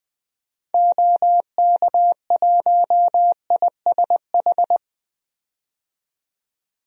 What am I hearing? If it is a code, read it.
Morse code, heard as a keyed tone, about 21 wpm, 700 Hz. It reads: OK1ISH
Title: DXCC-100 Call Signs - 6 Characters - 20wpm